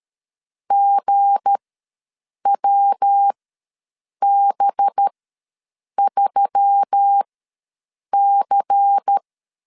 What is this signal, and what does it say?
Morse code, heard as a keyed tone, about 13 wpm, spaced as for 7 wpm, 800 Hz. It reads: GWB3C
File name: LEZIONI / 8 YJ45 / 8 YJ ws 12wpm 3342